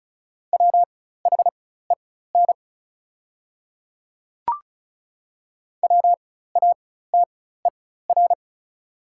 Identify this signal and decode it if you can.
Morse code, heard as a keyed tone, about 36 wpm, spaced as for 16 wpm, 700 Hz. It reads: WHEN WATER
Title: 4x Character Spacing - Top 100 Words - 35wpm